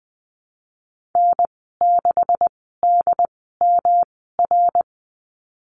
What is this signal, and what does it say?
Morse code, heard as a keyed tone, about 19 wpm, spaced as for 14 wpm, 700 Hz. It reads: N6DMR
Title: 154 Lesson 25g - Letter p - Call Signs 20wpm